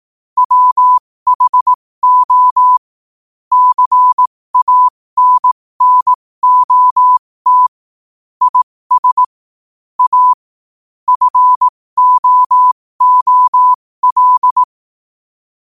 Morse code, heard as a keyed tone, about 18 wpm, spaced as for 15 wpm, 1000 Hz. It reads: WHO CANNOT IS A FOOL
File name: 15wpm